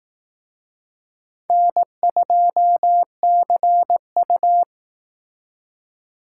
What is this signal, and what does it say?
Morse code, heard as a keyed tone, about 18 wpm, 700 Hz. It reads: N2CU
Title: Speed Racing - USA 1x2 Call Signs 15wpm